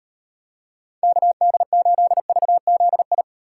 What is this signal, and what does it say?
Morse code, heard as a keyed tone, about 34 wpm, 700 Hz. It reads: KD8VZI